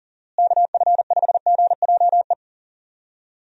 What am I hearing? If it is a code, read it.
Morse code, heard as a keyed tone, about 40 wpm, 700 Hz. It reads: KF5GJE
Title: USA Call Signs - 6 Characters - Repeated 3x - 40wpm